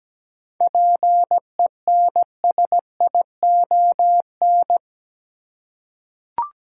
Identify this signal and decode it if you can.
Morse code, heard as a keyed tone, about 17 wpm, 700 Hz. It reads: PENSION
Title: Seven Letter Words - 17wpm